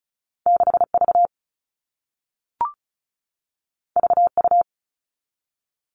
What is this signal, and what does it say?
Morse code, heard as a keyed tone, about 36 wpm, 700 Hz. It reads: 6V VU
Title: All Cut Numbers - 2-Digit 35wpm